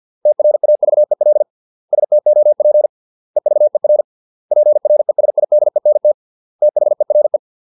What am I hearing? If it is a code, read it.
Morse code, heard as a keyed tone, about 50 wpm, 600 Hz. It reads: TRAVEL STOP EVER PRESIDENT THERE